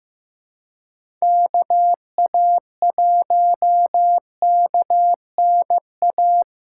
Morse code, heard as a keyed tone, about 15 wpm, 700 Hz. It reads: KA1KNA
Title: USA Call Signs - 6 Characters - 4x Character Spacing - 15wpm